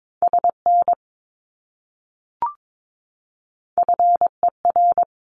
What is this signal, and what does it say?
Morse code, heard as a keyed tone, about 22 wpm, 700 Hz. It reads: SN FER
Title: Abbreviations Repeated 3x - 22wpm